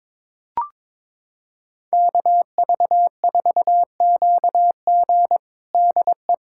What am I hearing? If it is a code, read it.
Morse code, heard as a keyed tone, about 22 wpm, 700 Hz. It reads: KV4QG DE
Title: USA Callsign DE Callsign - 5 Characters - 22wpm